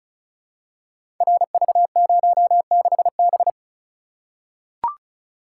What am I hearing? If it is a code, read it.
Morse code, heard as a keyed tone, about 36 wpm, 700 Hz. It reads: RV06B
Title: Code Groups - 5 Letter:Numbers - 35wpm